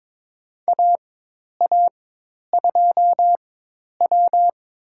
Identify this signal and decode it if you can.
Morse code, heard as a keyed tone, about 21 wpm, spaced as for 10 wpm, 700 Hz. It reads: AA2W
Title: USA Call Signs - 4x Character Spacing - 22wpm